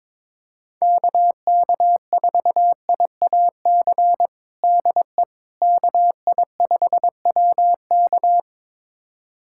Morse code, heard as a keyed tone, about 22 wpm, 700 Hz. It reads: KK4IAC DE KI5WK
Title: USA Callsign DE Callsign 22wpm